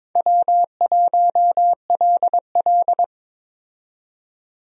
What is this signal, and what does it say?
Morse code, heard as a keyed tone, about 22 wpm, 700 Hz. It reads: W1LL
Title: USA Call Signs - 4 Characters - 22wpm